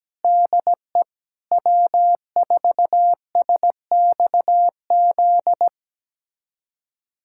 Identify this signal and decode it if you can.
Morse code, heard as a keyed tone, about 17 wpm, 700 Hz. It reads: DE W4SXZ
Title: USA Callsign DE Callsign - 5 Characters - 17wpm